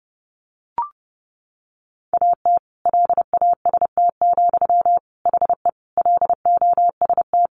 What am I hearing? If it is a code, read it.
Morse code, heard as a keyed tone, about 30 wpm, 700 Hz. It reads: AT LAST, HE LOST
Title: Sentences from Top 1400 Words 30wpm